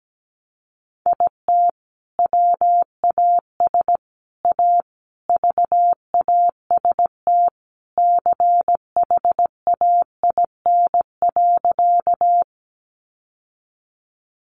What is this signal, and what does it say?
Morse code, heard as a keyed tone, about 17 wpm, 700 Hz. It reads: IT WAS A VAST CHAIN.
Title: Sentences from Top 1500 Words 17wpm